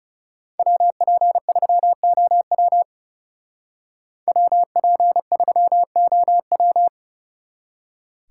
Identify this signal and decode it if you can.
Morse code, heard as a keyed tone, about 39 wpm, 700 Hz. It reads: WP3OW WP3OW